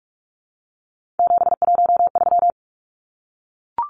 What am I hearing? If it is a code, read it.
Morse code, heard as a keyed tone, about 44 wpm, 700 Hz. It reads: 713
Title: Numbers - 3-Digit 45wpm